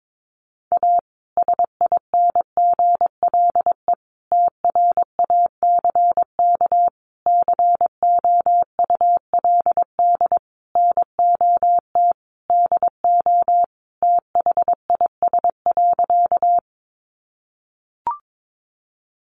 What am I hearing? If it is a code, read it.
Morse code, heard as a keyed tone, about 22 wpm, 700 Hz. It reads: A SINGLE TRACK COULD NOT DO THIS.
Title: Sentences from Top 1100 Words 22wpm